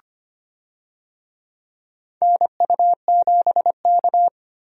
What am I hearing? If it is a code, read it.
Morse code, heard as a keyed tone, about 25 wpm, 700 Hz. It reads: NU7K
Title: USA Call Signs - 4 Characters - 25wpm